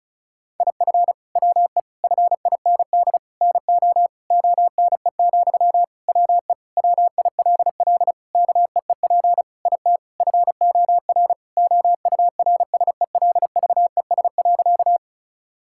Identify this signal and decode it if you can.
Morse code, heard as a keyed tone, about 36 wpm, 700 Hz. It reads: IF WE FIND NO ONE, WE WILL KEEP IT FOR OURSELVES.
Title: Sentences from Top 1600 Words 35wpm